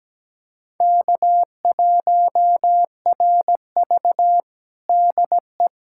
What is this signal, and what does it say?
Morse code, heard as a keyed tone, about 17 wpm, 700 Hz. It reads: K1RV DE
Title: Practice QSO - Exchange 4 - Ending - 17wpm